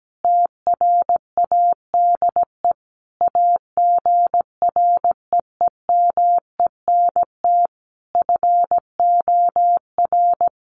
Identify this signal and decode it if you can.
Morse code, heard as a keyed tone, about 17 wpm, 700 Hz. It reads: TRADE AGREEMENT FOR
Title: Sentences from Top 1500 Words 17wpm